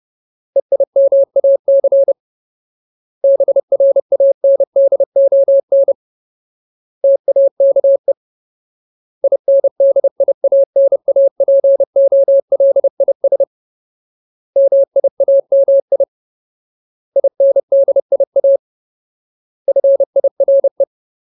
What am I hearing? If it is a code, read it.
Morse code, heard as a keyed tone, about 30 wpm, 550 Hz. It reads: EIMAC BRANDON TAKE INDIANAPOLIS MIAMI INDIA FIRE
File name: Ultimate Word List / 30WPM Ultimate